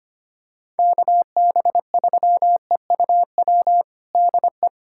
Morse code, heard as a keyed tone, about 25 wpm, 700 Hz. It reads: KB3EUW DE
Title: USA Callsign DE Callsign - 6 Characters - 25wpm